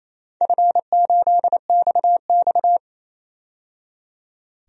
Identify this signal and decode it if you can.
Morse code, heard as a keyed tone, about 28 wpm, 700 Hz. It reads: F8XX
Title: DXCC-100 Call Signs - 4 Characters - 28wpm